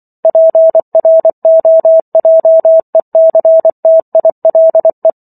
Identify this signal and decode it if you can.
Morse code, heard as a keyed tone, about 24 wpm, 650 Hz. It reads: PROJECTILE
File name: Chapter 15 - 24 WPM